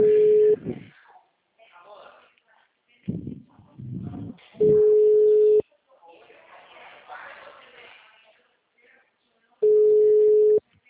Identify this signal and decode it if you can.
Ringback tone (Europe, 425 Hz), with speech.